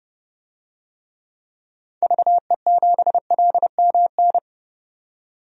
Morse code, heard as a keyed tone, about 30 wpm, 700 Hz. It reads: VE7LMN
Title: DXCC-100 Call Signs - 6 Characters - 30wpm